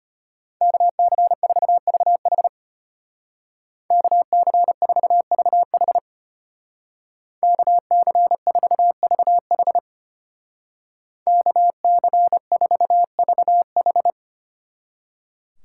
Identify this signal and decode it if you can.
Morse code, heard as a keyed tone, about 32 wpm, 700 Hz. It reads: KC4VH KC4VH KC4VH KC4VH